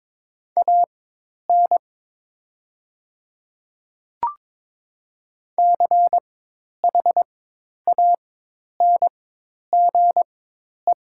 Morse code, heard as a keyed tone, about 22 wpm, spaced as for 10 wpm, 700 Hz. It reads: AN CHANGE
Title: 4x Character Spacing - Top 200 Words - 22wpm